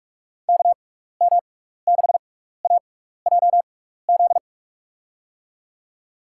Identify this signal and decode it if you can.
Morse code, heard as a keyed tone, about 44 wpm, spaced as for 15 wpm, 700 Hz. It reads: KM6AJZ
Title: USA Call Signs - 6x Character Spacing - 45wpm